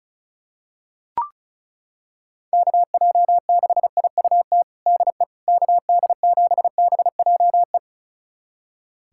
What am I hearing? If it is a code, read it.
Morse code, heard as a keyed tone, about 34 wpm, 700 Hz. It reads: KJ6IUT DE KD7BJE
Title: USA Callsign DE Callsign 35wpm